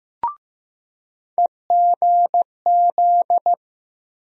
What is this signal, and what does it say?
Morse code, heard as a keyed tone, about 15 wpm, 700 Hz. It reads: EGZ
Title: USA Call Sign Postfixes - 3 Letters 15wpm